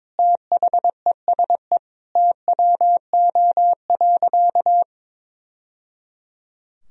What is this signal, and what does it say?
Morse code, heard as a keyed tone, about 22 wpm, 700 Hz. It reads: THESE TWO.
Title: Sentences from Top 1000 Words - Review Part 1 - 22wpm